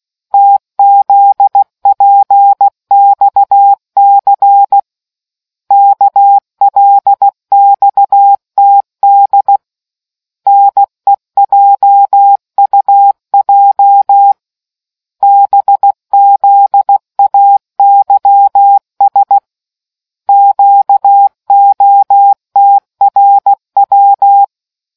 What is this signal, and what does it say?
Morse code, heard as a keyed tone, about 16 wpm, 800 Hz. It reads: TZPXC KLXTD NEJUJ BZAYS QOTRW